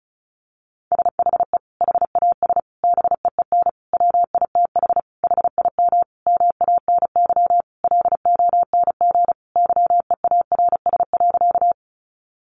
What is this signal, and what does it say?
Morse code, heard as a keyed tone, about 34 wpm, 700 Hz. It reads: SHE HAS BEEN WITH HIM MANY LONG YEARS.
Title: Sentences from Top 200 Words 35wpm